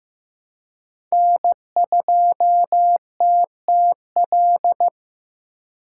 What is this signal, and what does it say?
Morse code, heard as a keyed tone, about 15 wpm, 700 Hz. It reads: N2TTL
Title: USA Call Signs - 5 Characters - 4x Character Spacing - 15wpm